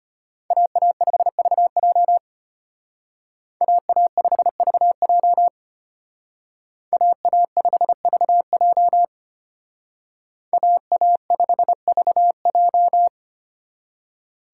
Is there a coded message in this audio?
Yes: Morse code from 0.5 to 13.1 s, heard as a keyed tone, about 32 wpm, 700 Hz. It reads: AA5VJ AA5VJ AA5VJ AA5VJ